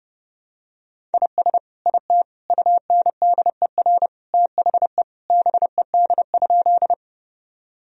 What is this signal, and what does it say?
Morse code, heard as a keyed tone, about 30 wpm, 700 Hz. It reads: IS IT UNDER THE BED?
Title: Sentences from Top 1000 Words - Review Part 1 - 30wpm